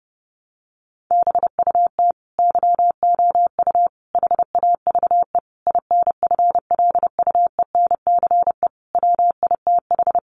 Morse code, heard as a keyed tone, about 30 wpm, 700 Hz. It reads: BUT YOU HAVE INFLUENCE WITH
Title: Sentences from Top 1100 Words 30wpm